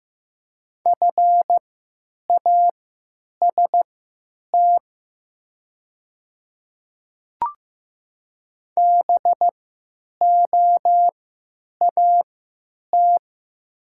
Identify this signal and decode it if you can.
Morse code, heard as a keyed tone, about 15 wpm, spaced as for 9 wpm, 700 Hz. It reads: FAST BOAT
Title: 3x Character Spacing - Top 500 Words - 15wpm